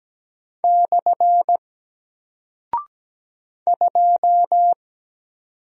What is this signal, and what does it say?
Morse code, heard as a keyed tone, about 17 wpm, 700 Hz. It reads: / 2
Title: Single Character (Letter:Number:Punctuation:Prosign) Alt 17wpm